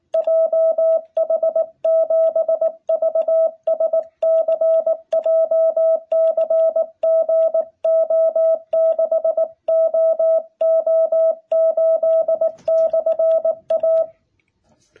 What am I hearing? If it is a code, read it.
Morse code, heard as a keyed tone, about 18 wpm, 660 Hz. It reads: JH7VSC JCG O6OO8/A
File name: jh8cbh / 2021cw